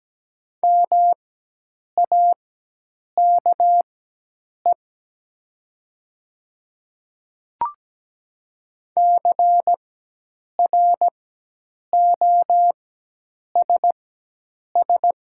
Morse code, heard as a keyed tone, about 17 wpm, spaced as for 8 wpm, 700 Hz. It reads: MAKE CROSS
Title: 4x Character Spacing - Top 300 Words - 17wpm